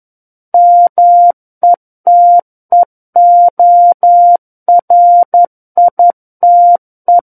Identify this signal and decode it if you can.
Morse code, heard as a keyed tone, about 11 wpm, 700 Hz. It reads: METEORITE
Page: SKCC - War of the Worlds - 02-Chapter 2 - 11 WPM